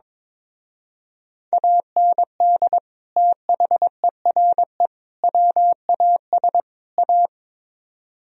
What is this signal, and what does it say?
Morse code, heard as a keyed tone, about 24 wpm, 700 Hz. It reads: AND THERE WAS A
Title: Speed-Racing - Sets of 4 Words 22wpm